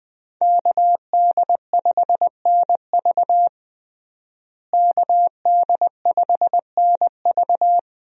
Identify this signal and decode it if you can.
Morse code, heard as a keyed tone, about 20 wpm, 700 Hz. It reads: KD5NV KD5NV
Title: USA Call Signs - 5 Characters - Repeated 3x - 20wpm